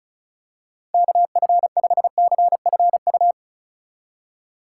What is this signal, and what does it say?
Morse code, heard as a keyed tone, about 36 wpm, 700 Hz. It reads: KF5CFU
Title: USA Call Signs - 6 Characters - 3x Character Spacing - 35wpm